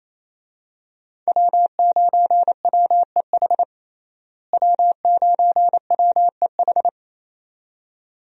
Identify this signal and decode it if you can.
Morse code, heard as a keyed tone, about 28 wpm, 700 Hz. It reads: W9WEH W9WEH
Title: USA Call Signs - 5 Characters - Repeated 2x - 28wpm